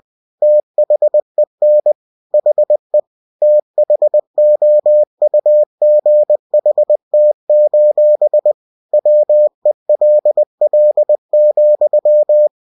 Morse code, heard as a keyed tone, about 20 wpm, 600 Hz. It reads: THEN HE THOUGHT: WELL,
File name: cw-books / winnie-the-pooh / winnie-20wpm-page-0050